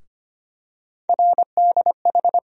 Morse code, heard as a keyed tone, about 25 wpm, 700 Hz. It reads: RDH
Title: USA Call Sign Postfixes - 3 Letters 25wpm